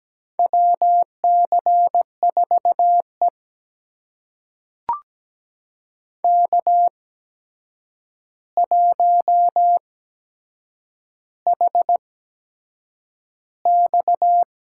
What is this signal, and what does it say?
Morse code, heard as a keyed tone, about 17 wpm, spaced as for 7 wpm, 700 Hz. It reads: WC4E K1HX
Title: USA Call Signs - 4 Characters - 8x Character Spacing - 17wpm